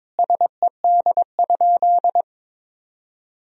Morse code, heard as a keyed tone, about 22 wpm, 700 Hz. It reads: SED?
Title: Sentences from Top 100 Words and Abbreviations 22wpm